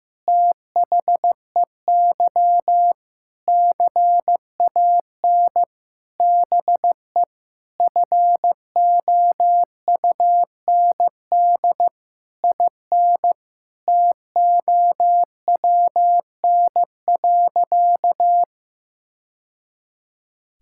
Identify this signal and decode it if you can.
Morse code, heard as a keyed tone, about 15 wpm, 700 Hz. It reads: THEY CAN BE FOUND IN TOWN.
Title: Sentences from Top 1000 Words - Review Part 6 - 15wpm